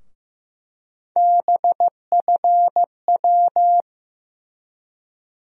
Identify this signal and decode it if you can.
Morse code, heard as a keyed tone, about 15 wpm, 700 Hz. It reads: BFW
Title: USA Call Sign Postfixes - 3 Letters 15wpm